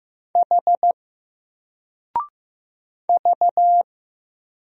Morse code, heard as a keyed tone, about 15 wpm, 700 Hz. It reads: H V
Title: Single Letter-Number 15wpm